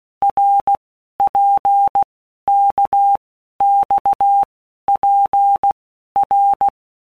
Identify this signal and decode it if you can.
Morse code, heard as a keyed tone, about 16 wpm, 800 Hz. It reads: R P K X P R
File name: Basic Letter Groups By Blocks / Groups@15 / RKPX@15